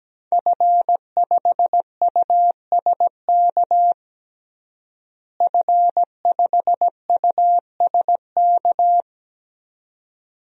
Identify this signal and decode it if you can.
Morse code, heard as a keyed tone, about 17 wpm, 700 Hz. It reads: F5USK F5USK
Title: DXCC-100 Call Signs - 5 Characters - Repeated 2x - 17wpm